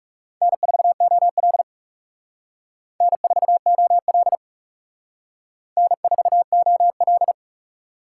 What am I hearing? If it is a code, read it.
Morse code, heard as a keyed tone, about 40 wpm, 700 Hz. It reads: N4OL N4OL N4OL